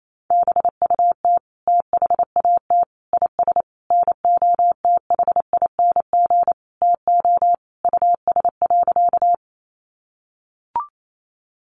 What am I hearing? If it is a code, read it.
Morse code, heard as a keyed tone, about 28 wpm, 700 Hz. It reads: BUT THAT IS NOTHING TO US.
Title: Sentences from Top 500 Words 28wpm